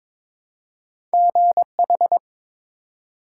Morse code, heard as a keyed tone, about 23 wpm, 700 Hz. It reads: GH
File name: ARRL Sections v2 - 22wpm